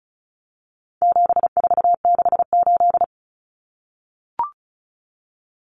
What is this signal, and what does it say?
Morse code, heard as a keyed tone, about 34 wpm, 700 Hz. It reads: 7468
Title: Numbers - 4-Digit 35wpm